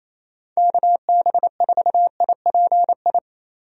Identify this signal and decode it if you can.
Morse code, heard as a keyed tone, about 27 wpm, 700 Hz. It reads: KB4IPI
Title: USA Call Signs - 6 Characters - Repeated 2x - 28wpm